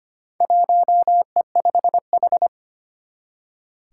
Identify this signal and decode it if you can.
Morse code, heard as a keyed tone, about 25 wpm, 700 Hz. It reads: 1E5H
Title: Four Challenging Letter-Number Combinations 25wpm